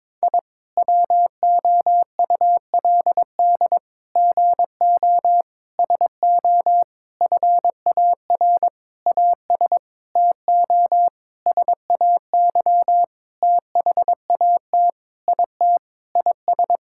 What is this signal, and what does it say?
Morse code, heard as a keyed tone, about 22 wpm, 700 Hz. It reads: I WOULD GO SO FAR AS TO SAY THAT IT IS
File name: Sentences with 9-Word Phrases - 22wpm